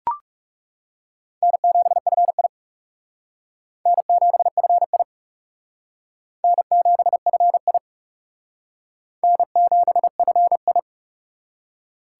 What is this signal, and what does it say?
Morse code, heard as a keyed tone, about 37 wpm, 700 Hz. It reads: N7FI N7FI N7FI N7FI
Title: Speed Racing - USA Call Signs 30wpm